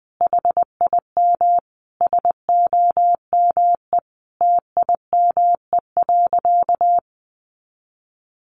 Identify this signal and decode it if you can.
Morse code, heard as a keyed tone, about 20 wpm, 700 Hz. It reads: HIM SOME TIME.
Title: Sentences from Top 300 Words 20wpm